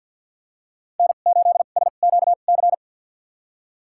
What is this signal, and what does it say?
Morse code, heard as a keyed tone, about 51 wpm, spaced as for 36 wpm, 700 Hz. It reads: N8SQC
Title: USA Call Signs - 5 Characters - 2x Character Spacing - 50wpm